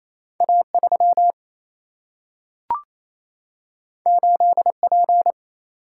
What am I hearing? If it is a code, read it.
Morse code, heard as a keyed tone, about 28 wpm, 700 Hz. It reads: A3 8P
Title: Single-Letter Single-Number 28wpm